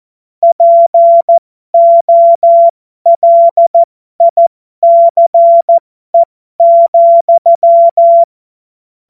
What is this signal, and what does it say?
Morse code, heard as a keyed tone, about 14 wpm, spaced as for 12 wpm, 680 Hz. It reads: POLICE,
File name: Morsekurs / Witze